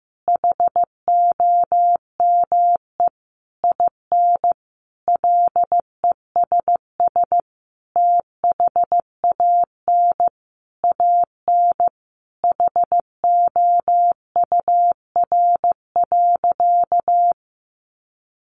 Morse code, heard as a keyed tone, about 15 wpm, 700 Hz. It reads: HOME IN LESS THAN AN HOUR.
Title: Sentences from Top 500 Words 15wpm